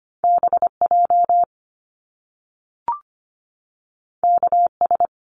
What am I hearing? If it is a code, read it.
Morse code, heard as a keyed tone, about 25 wpm, 700 Hz. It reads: BJ KS
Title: Top Two Letter Combinations in Words 25wpm